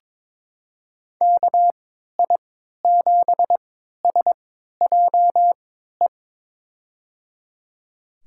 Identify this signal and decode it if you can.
Morse code, heard as a keyed tone, about 22 wpm, spaced as for 12 wpm, 700 Hz. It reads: KI7SJE
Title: USA Call Signs - 6 Characters - 3x Character Spacing - 22wpm